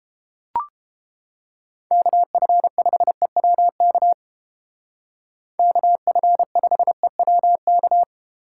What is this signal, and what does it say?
Morse code, heard as a keyed tone, about 30 wpm, 700 Hz. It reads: KF5EWK KF5EWK